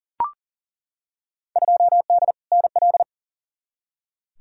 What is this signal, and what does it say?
Morse code, heard as a keyed tone, about 40 wpm, 700 Hz. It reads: 2D NL
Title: Field Day Exchanges Encore - 40wpm